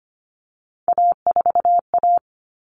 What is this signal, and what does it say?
Morse code, heard as a keyed tone, about 25 wpm, 700 Hz. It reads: A4A
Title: All Cut Numbers - 3-Digit 25wpm